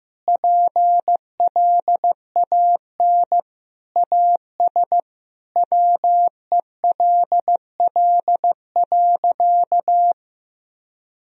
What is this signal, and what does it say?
Morse code, heard as a keyed tone, about 16 wpm, 700 Hz. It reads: PLAN AS WELL.
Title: Sentences from Top 1000 Words - Review Part 2 - 15wpm